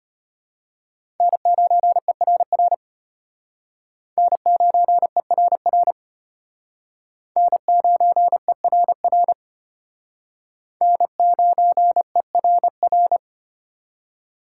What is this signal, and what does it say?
Morse code, heard as a keyed tone, about 32 wpm, 700 Hz. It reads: N9ERR N9ERR N9ERR N9ERR